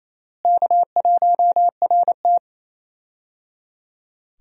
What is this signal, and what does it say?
Morse code, heard as a keyed tone, about 28 wpm, 700 Hz. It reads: K1RT
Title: USA Call Signs - 4 Characters - Repeated 3x - 28wpm